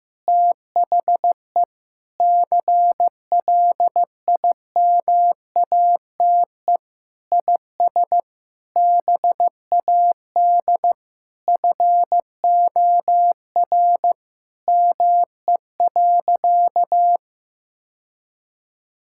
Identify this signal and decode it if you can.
Morse code, heard as a keyed tone, about 15 wpm, 700 Hz. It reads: THE CLIMATE IS BAD FOR ME.
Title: Sentences from Top 1600 Words 15wpm